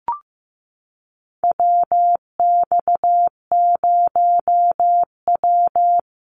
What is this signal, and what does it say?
Morse code, heard as a keyed tone, about 15 wpm, 700 Hz. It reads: WX0W
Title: USA Call Signs Repeated 2x 15wpm